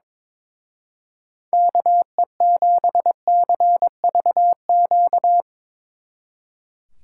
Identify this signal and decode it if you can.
Morse code, heard as a keyed tone, about 24 wpm, 700 Hz. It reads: KE7CVQ
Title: Speed Racing - USA 2x3 Call Signs 22wpm